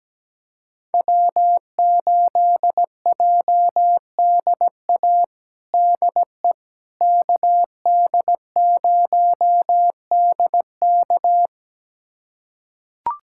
Morse code, heard as a keyed tone, about 17 wpm, 700 Hz. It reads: W8JDA DE KD0DK
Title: USA Callsign DE Callsign - 5 Characters - 17wpm